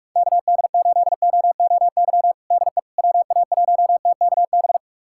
Morse code, heard as a keyed tone, about 46 wpm, 700 Hz. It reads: KD8OOY DE WA1TKB
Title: USA Callsign DE Callsign - 6 Characters - 45wpm